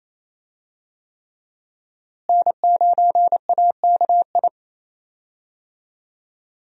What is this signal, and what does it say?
Morse code, heard as a keyed tone, about 28 wpm, 700 Hz. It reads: N9AKI
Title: USA Call Signs - 5 Characters - Repeated 3x - 28wpm